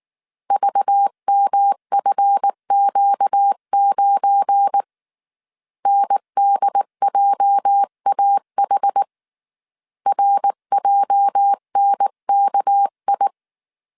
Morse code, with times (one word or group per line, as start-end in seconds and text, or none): 0.5-4.8 VMFQ9
5.9-9.0 NDJAH
10.1-13.3 RJNKI